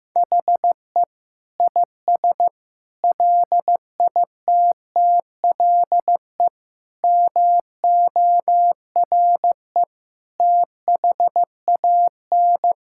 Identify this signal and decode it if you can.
Morse code, heard as a keyed tone, about 15 wpm, 700 Hz. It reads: HE IS LITTLE MORE THAN
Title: Sentences from Top 1600 Words 15wpm